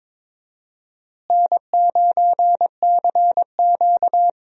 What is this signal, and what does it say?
Morse code, heard as a keyed tone, about 22 wpm, 700 Hz. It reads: N9CQ
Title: USA Call Signs - 4 Characters - Repeated 2x - 22wpm